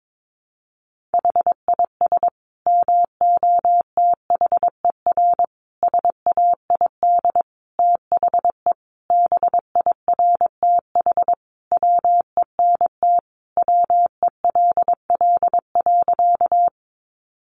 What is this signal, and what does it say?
Morse code, heard as a keyed tone, about 22 wpm, 700 Hz. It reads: HIS MOTHER SAID THE BIRTH WENT WELL.
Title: Sentences from Top 800 Words 22wpm